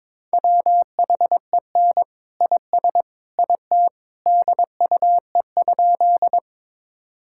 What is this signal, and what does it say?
Morse code, heard as a keyed tone, about 22 wpm, 700 Hz. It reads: WHEN IS IT DUE?
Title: Sentences from Top 1000 Words - Review Part 6 - 22wpm